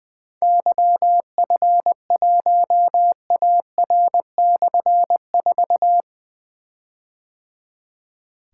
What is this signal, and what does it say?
Morse code, heard as a keyed tone, about 20 wpm, 700 Hz. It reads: YF1AR/4
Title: Difficult Call Signs 20wpm